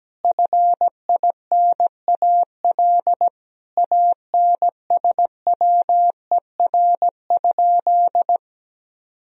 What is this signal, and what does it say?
Morse code, heard as a keyed tone, about 17 wpm, 700 Hz. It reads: FINAL ANSWER?
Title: Sentences from Top 1000 Words - Review Part 2 - 17wpm